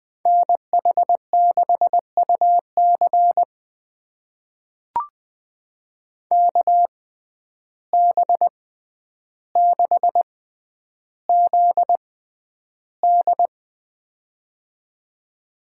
Morse code, heard as a keyed tone, about 20 wpm, spaced as for 10 wpm, 700 Hz. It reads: NH6UC KB6ZD
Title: USA Call Signs - 5 Characters - 6x Character Spacing - 20wpm